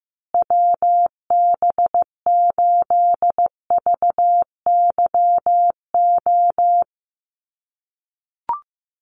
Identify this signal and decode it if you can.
Morse code, heard as a keyed tone, about 15 wpm, 700 Hz. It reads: WB8VYO
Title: USA Call Signs Repeated 3x 15wpm